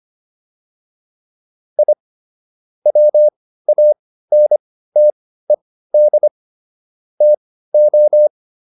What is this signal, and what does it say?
Morse code, heard as a keyed tone, about 25 wpm, spaced as for 15 wpm, 600 Hz. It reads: I WANTED TO
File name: WD404 15